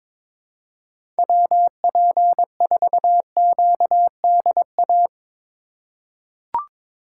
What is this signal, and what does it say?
Morse code, heard as a keyed tone, about 22 wpm, 700 Hz. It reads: WP4QDA
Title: USA Call Signs - 6 Characters - 4x Character Spacing - 22wpm